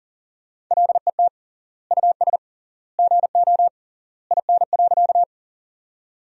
Morse code, heard as a keyed tone, about 37 wpm, 700 Hz. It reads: LET US GO IN.
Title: 3x Word-Spacing - Sentences from Top 300 Words - 40wpm